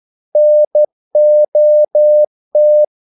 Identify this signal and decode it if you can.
Morse code, heard as a keyed tone, about 12 wpm, 600 Hz. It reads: NOT
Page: English quotations in CW (telegraphy - morse)